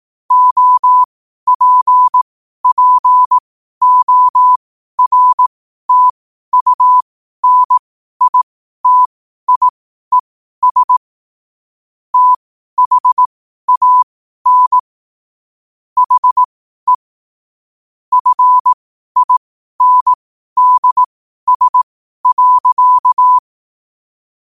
Morse code, heard as a keyed tone, about 18 wpm, spaced as for 12 wpm, 1000 Hz. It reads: OPPORTUNITIES THAN HE FINDS.
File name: qotdfiles / 12wpm